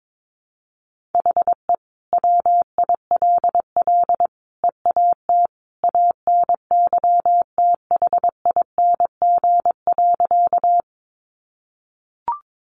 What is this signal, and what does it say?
Morse code, heard as a keyed tone, about 22 wpm, 700 Hz. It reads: HE WILL EAT ANYTHING.
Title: Sentences from Top 1000 Words 22wpm